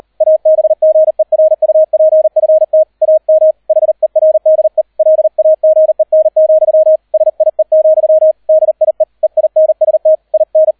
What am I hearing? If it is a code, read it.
Morse code, heard as a keyed tone, about 38 wpm, 630 Hz. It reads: ABGERUPFT AM HERDE LAGEN, SIE, DIE EINST IN